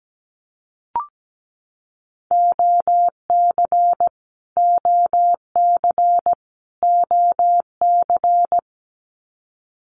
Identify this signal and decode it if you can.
Morse code, heard as a keyed tone, about 17 wpm, 700 Hz. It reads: OC OC OC
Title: Abbreviations Repeated 3x - 17wpm